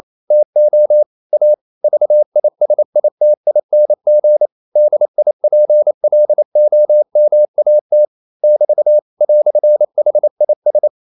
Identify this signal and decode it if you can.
Morse code, heard as a keyed tone, about 28 wpm, 600 Hz. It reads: TO A VISITING DIPLOMAT <BT> "HIS
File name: CW / Practice / EnglishQuotations0128